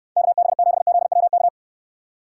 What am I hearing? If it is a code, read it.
Morse code, heard as a keyed tone, about 66 wpm, 700 Hz. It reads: KB7ZWK